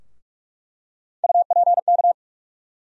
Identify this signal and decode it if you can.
Morse code, heard as a keyed tone, about 45 wpm, 700 Hz. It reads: UPK
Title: USA Call Sign Postfixes - 3 Letters 45wpm